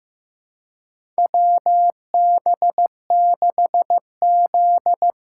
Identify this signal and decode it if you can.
Morse code, heard as a keyed tone, about 15 wpm, 700 Hz. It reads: WB6Z
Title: USA Call Signs - 6 Characters - 6x Character Spacing - 15wpm